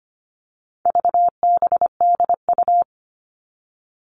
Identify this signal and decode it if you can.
Morse code, heard as a keyed tone, about 25 wpm, 700 Hz. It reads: VBDU
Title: All Cut Numbers - 4-Digit 25wpm